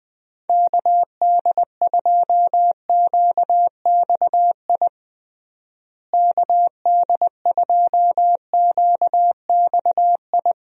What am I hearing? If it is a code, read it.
Morse code, heard as a keyed tone, about 20 wpm, 700 Hz. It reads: KD2QXI KD2QXI
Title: USA Call Signs - 6 Characters - Repeated 3x - 20wpm